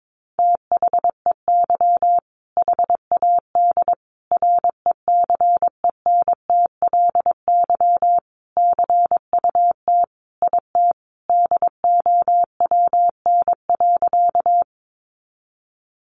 Morse code, heard as a keyed tone, about 22 wpm, 700 Hz. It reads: THEY HAD RECENTLY CUT IT DOWN.
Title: Sentences from Top 1200 Words 22wpm